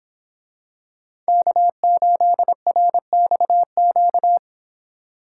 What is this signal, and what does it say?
Morse code, heard as a keyed tone, about 26 wpm, 700 Hz. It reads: K8RXQ